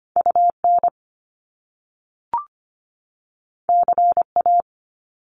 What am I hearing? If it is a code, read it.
Morse code, heard as a keyed tone, about 25 wpm, 700 Hz. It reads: UN CA
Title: Top Two Letter Combinations in Words 25wpm